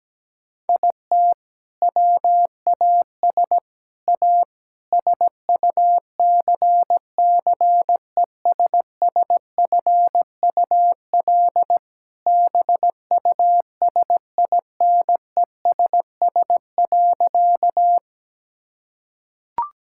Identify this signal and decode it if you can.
Morse code, heard as a keyed tone, about 17 wpm, 700 Hz. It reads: IT WAS A SUCCESSFUL BUSINESS.
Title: Sentences from Top 1000 Words - Review Part 6 - 17wpm